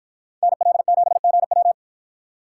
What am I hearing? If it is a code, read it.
Morse code, heard as a keyed tone, about 53 wpm, 700 Hz. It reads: NL7GW